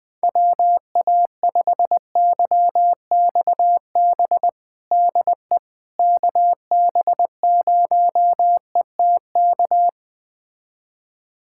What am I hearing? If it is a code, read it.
Morse code, heard as a keyed tone, about 20 wpm, 700 Hz. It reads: WA5YXB DE KB0ETK
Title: USA Callsign DE Callsign - 6 Characters - 20wpm